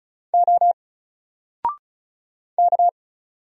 Morse code, heard as a keyed tone, about 34 wpm, 700 Hz. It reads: O K
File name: Single-Letter-Number-35wpm